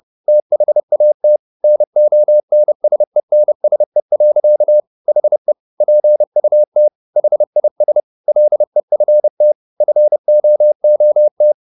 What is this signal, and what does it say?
Morse code, heard as a keyed tone, about 30 wpm, 600 Hz. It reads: THAT NONSENSE. HE PUT HIS LEFT FOOT
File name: the old man and the sea / the old man and the sea 1 30wpm